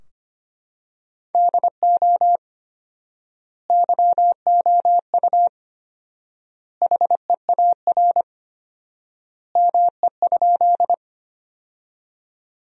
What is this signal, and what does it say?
Morse code, heard as a keyed tone, about 25 wpm, 700 Hz. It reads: DO YOU HEAR ME?
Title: 4x Word-Spacing - Sentences from Top 300 Words - 25wpm